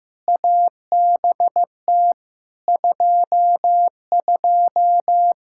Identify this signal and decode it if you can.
Morse code, heard as a keyed tone, about 15 wpm, 700 Hz. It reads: ABT 22
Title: Practice QSO - Exchange 3 - More Info - 15wpm